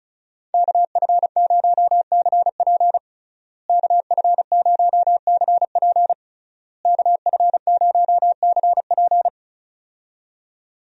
Morse code, heard as a keyed tone, about 34 wpm, 700 Hz. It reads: KF0CP KF0CP KF0CP
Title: USA Call Signs - 5 Characters - Repeated 3x - 35wpm